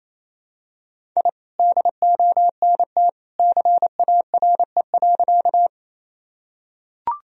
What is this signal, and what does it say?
Morse code, heard as a keyed tone, about 28 wpm, 700 Hz. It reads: I DONT CARE.
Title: Sentences from Top 1000 Words - Review Part 1 - 28wpm